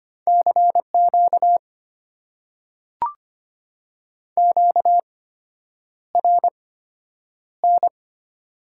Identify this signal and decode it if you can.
Morse code, heard as a keyed tone, about 25 wpm, spaced as for 9 wpm, 700 Hz. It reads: CQ QRN
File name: Top 90 QSO Elements - 8x Character Spacing - 25wpm